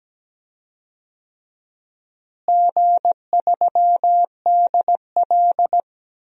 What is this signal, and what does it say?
Morse code, heard as a keyed tone, about 17 wpm, 700 Hz. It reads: G3DL
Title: DXCC-100 Call Signs - 4 Characters - Repeated 3x - 17wpm